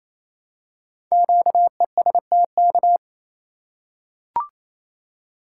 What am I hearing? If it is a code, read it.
Morse code, heard as a keyed tone, about 29 wpm, 700 Hz. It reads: QESTK
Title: Code Groups - 5 Letters - 28wpm